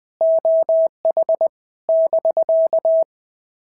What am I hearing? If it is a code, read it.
Morse code, heard as a keyed tone, about 18 wpm, 650 Hz. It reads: OH <BK>
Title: POTA QSO Practice - 20wpm